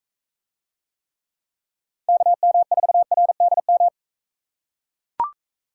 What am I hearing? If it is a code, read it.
Morse code, heard as a keyed tone, about 40 wpm, 700 Hz. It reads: KM4RDM